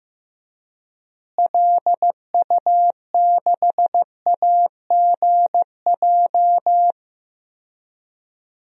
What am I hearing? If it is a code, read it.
Morse code, heard as a keyed tone, about 15 wpm, 700 Hz. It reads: LU6AGJ
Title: DXCC-100 Call Signs - 6 Characters - Repeated 3x - 15wpm